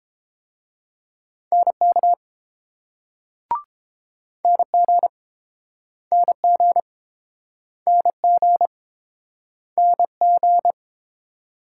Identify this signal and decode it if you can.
Morse code, heard as a keyed tone, about 30 wpm, 700 Hz. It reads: NK NG NG NG NG